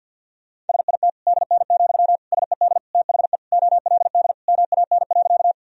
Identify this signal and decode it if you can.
Morse code, heard as a keyed tone, about 49 wpm, 700 Hz. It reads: SIT DN, SED THE OLD MAN.
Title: Sentences from Top 500 Words and Abbreviations 50wpm